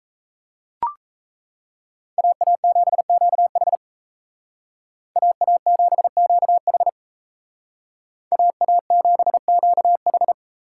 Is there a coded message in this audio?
Yes: Morse code from 2.2 to 10.3 s, heard as a keyed tone, about 37 wpm, 700 Hz. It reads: AA7QH AA7QH AA7QH